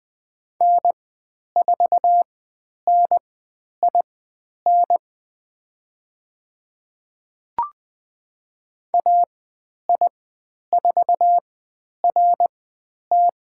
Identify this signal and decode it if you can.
Morse code, heard as a keyed tone, about 20 wpm, spaced as for 10 wpm, 700 Hz. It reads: N4NIN AI4RT
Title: 102 Lesson 17g - Letter u - Call Signs 20wpm